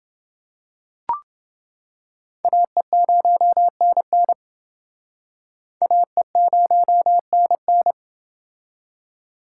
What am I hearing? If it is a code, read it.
Morse code, heard as a keyed tone, about 29 wpm, 700 Hz. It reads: AE0NN AE0NN